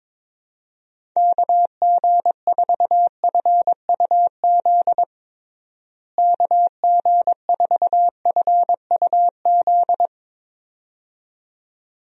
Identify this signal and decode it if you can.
Morse code, heard as a keyed tone, about 22 wpm, 700 Hz. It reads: KG4FUZ KG4FUZ
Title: USA Call Signs - 6 Characters - Repeated 2x - 22wpm